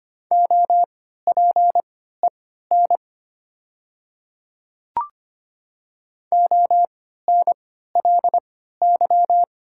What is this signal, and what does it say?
Morse code, heard as a keyed tone, about 25 wpm, spaced as for 14 wpm, 700 Hz. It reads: OPEN ONLY
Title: 3x Character Spacing - Top 300 Words - 25wpm